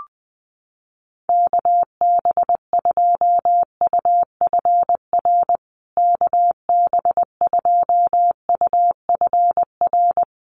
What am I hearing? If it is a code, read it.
Morse code, heard as a keyed tone, about 20 wpm, 700 Hz. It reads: KB2UFR KB2UFR
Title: USA Call Signs Repeated 3x 20wpm